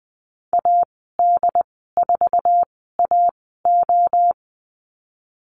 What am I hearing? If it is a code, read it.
Morse code, heard as a keyed tone, about 20 wpm, spaced as for 15 wpm, 700 Hz. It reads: AD4AO
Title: 076 Lesson 13g - Letter L - Call Signs 20wpm